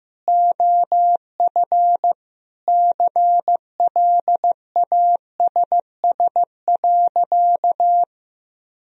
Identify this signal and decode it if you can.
Morse code, heard as a keyed tone, about 15 wpm, 700 Hz. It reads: OF CLASS.
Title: Sentences from Top 1000 Words - Review Part 2 - 15wpm